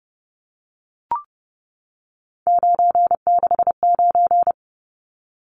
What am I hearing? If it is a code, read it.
Morse code, heard as a keyed tone, about 30 wpm, 700 Hz. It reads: 969
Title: Numbers - 3-Digit 30wpm